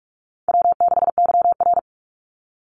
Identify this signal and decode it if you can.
Morse code, heard as a keyed tone, about 44 wpm, 700 Hz. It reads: W6YR
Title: Call Signs Repeated 3x - 45wpm